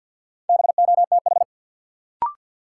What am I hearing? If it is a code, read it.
Morse code, heard as a keyed tone, about 53 wpm, 700 Hz. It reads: BOTH
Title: Sentences from Top 300 Words - New Word Drill - 50wpm